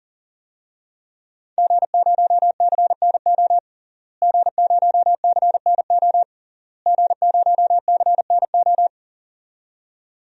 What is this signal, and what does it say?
Morse code, heard as a keyed tone, about 40 wpm, 700 Hz. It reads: G0CNO G0CNO G0CNO
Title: DXCC-100 Call Signs - 5 Characters - Repeated 3x - 40wpm